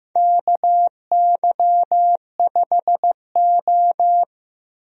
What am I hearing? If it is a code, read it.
Morse code, heard as a keyed tone, about 15 wpm, 700 Hz. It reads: KY5O
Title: USA Call Signs - 4 Characters - 15wpm